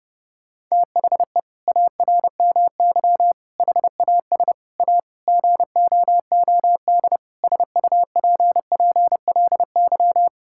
Morse code, heard as a keyed tone, about 30 wpm, 700 Hz. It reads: THE ARMY HAS A GOOD SUPPLY
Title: Sentences from Top 1000 Words - Review Part 5 - 30wpm